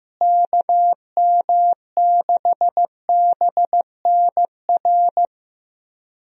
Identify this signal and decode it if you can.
Morse code, heard as a keyed tone, about 15 wpm, 700 Hz. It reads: KM6BNR
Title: USA Call Signs - 6 Characters - Repeated 3x - 15wpm